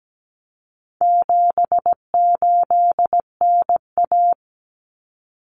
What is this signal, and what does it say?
Morse code, heard as a keyed tone, about 17 wpm, 700 Hz. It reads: 78NA
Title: Common Cut Numbers - 4-Digit 17wpm